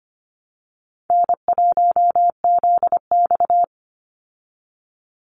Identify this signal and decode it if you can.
Morse code, heard as a keyed tone, about 25 wpm, 700 Hz. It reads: N1ZX
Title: Easy Call Signs Repeated 2x - 25wpm